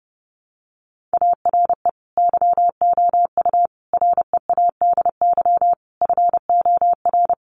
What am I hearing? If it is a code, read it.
Morse code, heard as a keyed tone, about 28 wpm, 700 Hz. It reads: ARE YOU READY FOR
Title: Sentences from Top 1500 Words 30wpm